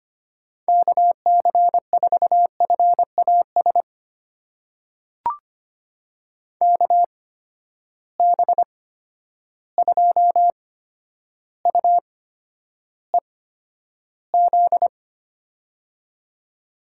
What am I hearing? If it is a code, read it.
Morse code, heard as a keyed tone, about 25 wpm, spaced as for 11 wpm, 700 Hz. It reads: KC4FAS KB2UEZ
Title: USA Call Signs - 6 Characters - 8x Character Spacing - 25wpm